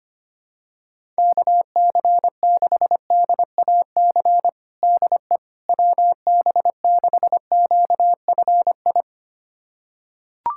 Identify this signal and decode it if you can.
Morse code, heard as a keyed tone, about 25 wpm, 700 Hz. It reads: KC6DAC DE WB6QFI
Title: USA Callsign DE Callsign - 6 Characters - 25wpm